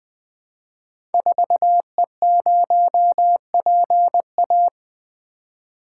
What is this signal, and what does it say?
Morse code, heard as a keyed tone, about 20 wpm, 700 Hz. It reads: 4E0PA
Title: Code Groups - 5 Letter:Numbers - 20wpm